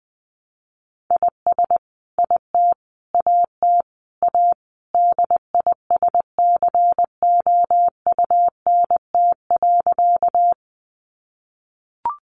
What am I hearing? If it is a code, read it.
Morse code, heard as a keyed tone, about 20 wpm, 700 Hz. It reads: IS IT AT A DISCOUNT.
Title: Sentences from Top 1200 Words 20wpm